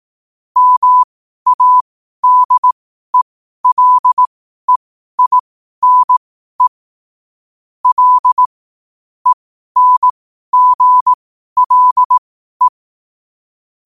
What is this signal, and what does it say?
Morse code, heard as a keyed tone, about 18 wpm, spaced as for 12 wpm, 1000 Hz. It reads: MADELEINE L ENGLE
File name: qotdfiles / 12wpm